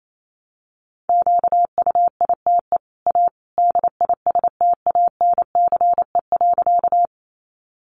Morse code, heard as a keyed tone, about 28 wpm, 700 Hz. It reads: QUITE A DISTANCE.
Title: Sentences from Top 1200 Words 28wpm